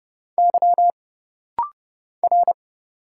Morse code, heard as a keyed tone, about 30 wpm, 700 Hz. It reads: Y R
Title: Single Letter-Number Alt - Rapid-Fire 30wpm